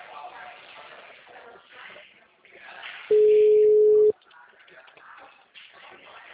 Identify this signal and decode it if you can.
Ringback tone (Europe, 425 Hz), with speech.
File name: out- -2019- - - .44271